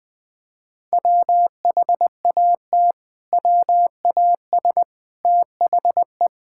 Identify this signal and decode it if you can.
Morse code, heard as a keyed tone, about 20 wpm, 700 Hz. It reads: WHAT WAS THE
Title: Sentences from Top 1000 Words - Review Part 5 - 20wpm